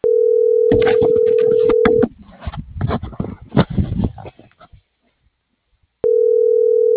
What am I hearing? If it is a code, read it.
Ringback tone (North America), 440 + 480 Hz.